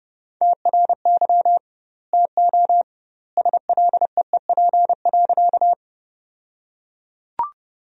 Morse code, heard as a keyed tone, about 30 wpm, 700 Hz. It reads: TRY TO SLEEP.
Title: 2x Word-Spacing - Sentences from Top 500 Words - 30wpm